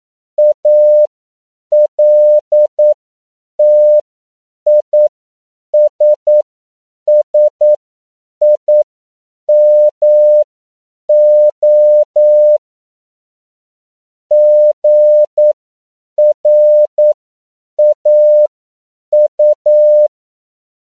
Morse code, heard as a keyed tone, about 9 wpm, spaced as for 7 wpm, 600 Hz. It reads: ALTISSIMO GRAU
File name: CW / 10PPM